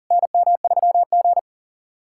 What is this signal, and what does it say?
Morse code, heard as a keyed tone, about 43 wpm, 690 Hz. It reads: NM3G
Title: SOTA QSO Practice 40wpm